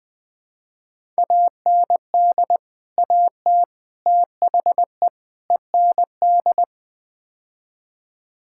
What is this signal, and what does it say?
Morse code, heard as a keyed tone, about 21 wpm, 700 Hz. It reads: AND AT THE END
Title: Build A Phrase - 5-Word Phrases - 20wpm